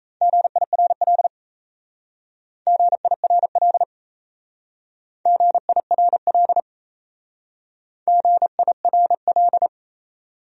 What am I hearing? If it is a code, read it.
Morse code, heard as a keyed tone, about 34 wpm, 700 Hz. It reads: GIRL GIRL GIRL GIRL
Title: Speed-Racing Top 500 Words 28wpm